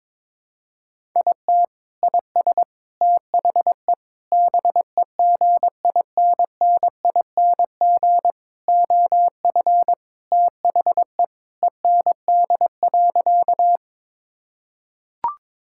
Morse code, heard as a keyed tone, about 21 wpm, 700 Hz. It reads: IT IS THE BEGINNING OF THE END.
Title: Sentences from Top 1000 Words - Review Part 5 - 22wpm